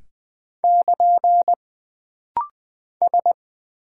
Morse code, heard as a keyed tone, about 20 wpm, 700 Hz. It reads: <KN> S
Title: Single Character (Letter:Number:Punctuation:Prosign) - Rapid-Fire - 20wpm